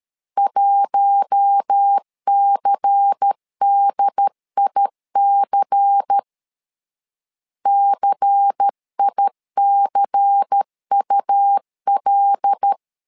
Morse code, heard as a keyed tone, about 13 wpm, 800 Hz. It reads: 1CDIC CICUL
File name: LEZIONI / 5 LBVC / 5 LBVC 12wpm 1955